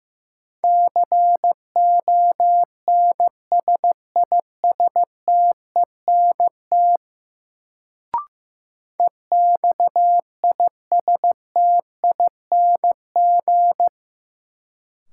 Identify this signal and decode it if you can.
Morse code, heard as a keyed tone, about 15 wpm, 700 Hz. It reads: CONSISTENT EXISTING
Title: Sentences from Top 1500 Words - New Word Drill - 15wpm